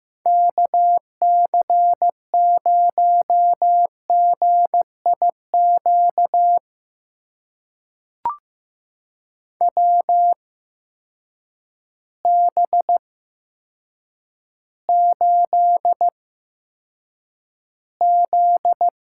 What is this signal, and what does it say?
Morse code, heard as a keyed tone, about 15 wpm, spaced as for 8 wpm, 700 Hz. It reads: KC0GIQ WB8Z
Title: USA Call Signs - 8x Character Spacing - 15wpm